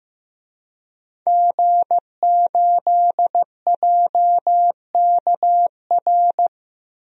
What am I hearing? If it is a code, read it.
Morse code, heard as a keyed tone, about 15 wpm, 700 Hz. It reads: G8JKR
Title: DXCC-100 Call Signs - 5 Characters - 15wpm